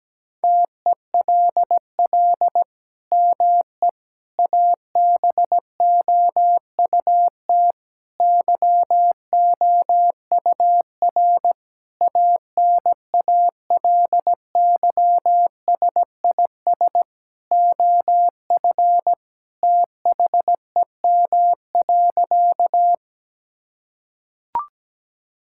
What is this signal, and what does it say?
Morse code, heard as a keyed tone, about 17 wpm, 700 Hz. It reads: TELL ME ABOUT YOUR ANALYSIS OF THEM.
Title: Sentences from Top 1000 Words - Review Part 5 - 17wpm